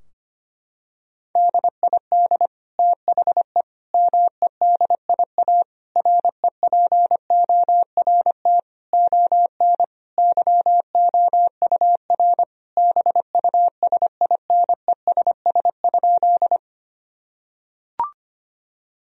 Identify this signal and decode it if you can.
Morse code, heard as a keyed tone, about 25 wpm, 700 Hz. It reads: DID THE MEDIA REPORT ON YOUR BUSINESS?
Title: Sentences from Top 1000 Words - Review Part 4 - 25wpm